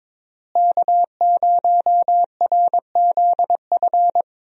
Morse code, heard as a keyed tone, about 22 wpm, 700 Hz. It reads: K0RZF
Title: USA Call Signs - 5 Characters - Repeated 2x - 22wpm